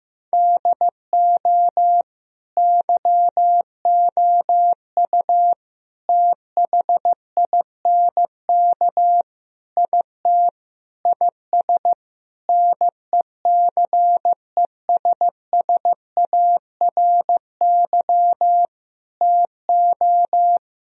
Morse code, heard as a keyed tone, about 15 wpm, 700 Hz. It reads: DO YOU THINK IT IS NECESSARY TO
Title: Sentences from Top 1000 Words - Review Part 1 - 15wpm